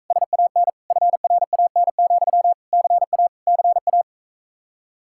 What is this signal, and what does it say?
Morse code, heard as a keyed tone, about 45 wpm, 700 Hz. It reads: SAN FRAN, CA CA